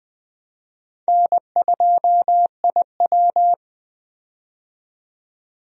Morse code, heard as a keyed tone, about 20 wpm, 700 Hz. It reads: N2IW
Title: USA Call Signs - 4 Characters - 20wpm